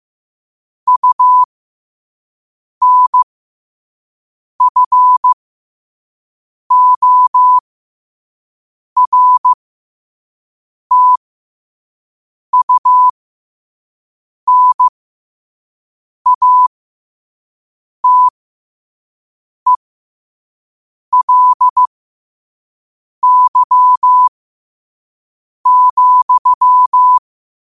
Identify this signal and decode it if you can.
Morse code, heard as a keyed tone, about 15 wpm, spaced as for 5 wpm, 1000 Hz. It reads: UNFORTUNATELY,
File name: qotdfiles / 5wpm